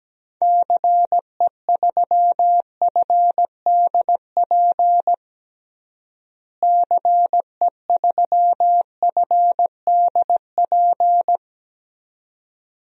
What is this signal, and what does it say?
Morse code, heard as a keyed tone, about 17 wpm, 700 Hz. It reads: CE3FDP CE3FDP
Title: DXCC-100 Call Signs - 6 Characters - Repeated 2x - 17wpm